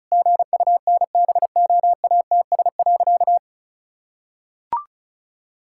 Morse code, heard as a keyed tone, about 36 wpm, 700 Hz. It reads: GUNBOATS.
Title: Pangram Challenge v2 - 35wpm